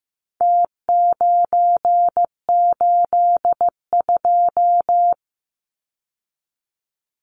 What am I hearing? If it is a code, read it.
Morse code, heard as a keyed tone, about 15 wpm, 700 Hz. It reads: T982
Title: Numbers - Contest Style - 4-Digit 15wpm